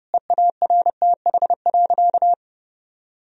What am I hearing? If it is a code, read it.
Morse code, heard as a keyed tone, about 30 wpm, 700 Hz. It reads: EARTH.
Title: Sentences from Top 1000 Words - Review Part 3 - 30wpm